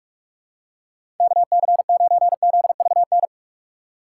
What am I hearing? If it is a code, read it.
Morse code, heard as a keyed tone, about 44 wpm, 700 Hz. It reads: KC9ZVN
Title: USA Call Signs - 6 Characters - 6x Character Spacing - 45wpm